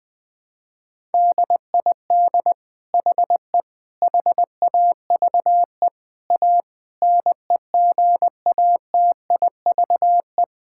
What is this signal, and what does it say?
Morse code, heard as a keyed tone, about 20 wpm, 700 Hz. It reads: DID HE HAVE A NEGATIVE
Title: Sentences from Top 1000 Words - Review Part 6 - 20wpm